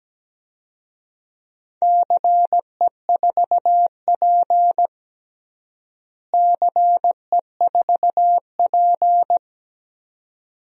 Morse code, heard as a keyed tone, about 17 wpm, 700 Hz. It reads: CE4P CE4P
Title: DXCC-100 Call Signs - 4 Characters - Repeated 2x - 17wpm